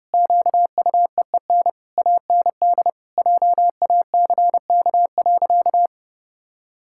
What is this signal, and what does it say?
Morse code, heard as a keyed tone, about 32 wpm, 700 Hz. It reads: QUEEN AND JACK.
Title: Pangram Challenge v2 - 30wpm